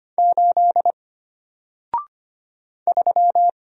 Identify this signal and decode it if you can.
Morse code, heard as a keyed tone, about 25 wpm, 700 Hz. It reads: 8 3
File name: Single-Letter-Number-25wpm